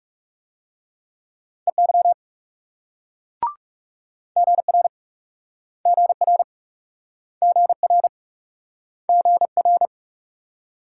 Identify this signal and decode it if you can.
Morse code, heard as a keyed tone, about 39 wpm, 700 Hz. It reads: EY GR GR GR GR